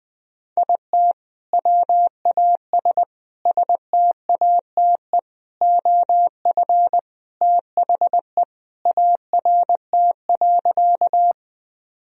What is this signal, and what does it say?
Morse code, heard as a keyed tone, about 20 wpm, 700 Hz. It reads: IT WAS STATE OF THE ART.
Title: Sentences from Top 1000 Words - Review Part 3 - 20wpm